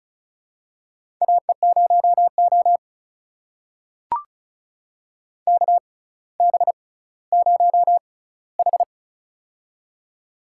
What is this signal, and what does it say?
Morse code, heard as a keyed tone, about 36 wpm, spaced as for 18 wpm, 700 Hz. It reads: AE0O KB0H
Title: USA Call Signs - 4 Characters - 6x Character Spacing - 35wpm